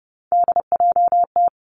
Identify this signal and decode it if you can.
Morse code, heard as a keyed tone, about 30 wpm, 700 Hz. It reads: DJT
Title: Three Challenging Letter-Number Combinations 30wpm